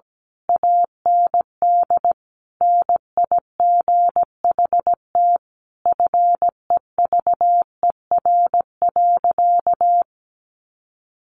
Morse code, heard as a keyed tone, about 17 wpm, 700 Hz. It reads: AND NIGHT FEVER.
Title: A Sound Of Thunder 17wpm